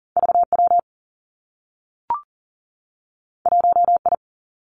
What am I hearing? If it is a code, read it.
Morse code, heard as a keyed tone, about 41 wpm, 700 Hz. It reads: V W 1 I
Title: Two Challenging Letter-Number Combinations 40wpm